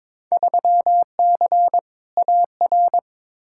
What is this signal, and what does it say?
Morse code, heard as a keyed tone, about 22 wpm, 700 Hz. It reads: 3C AR
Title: Field Day Exchanges Encore - 22wpm